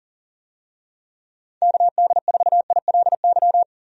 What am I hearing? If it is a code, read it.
Morse code, heard as a keyed tone, about 38 wpm, 700 Hz. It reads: KD4ILY
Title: USA Call Signs - 6 Characters - 40wpm